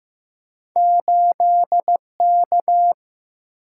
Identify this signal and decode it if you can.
Morse code, heard as a keyed tone, about 15 wpm, 700 Hz. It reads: 8K
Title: Single-Letter Single-Number 15wpm